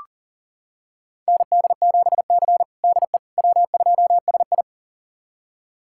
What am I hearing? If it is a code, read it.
Morse code, heard as a keyed tone, about 40 wpm, 700 Hz. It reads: ND7C DE W2SI
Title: USA Callsign DE Callsign - 4 Characters - 40wpm